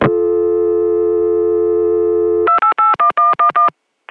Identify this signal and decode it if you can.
DTMF, heard as telephone touch tones, tones of about 120 ms. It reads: 2881111